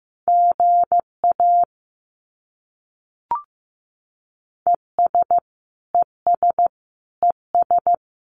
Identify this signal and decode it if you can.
Morse code, heard as a keyed tone, about 15 wpm, 700 Hz. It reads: GA ES ES ES
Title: Abbreviations Repeated 3x - 15wpm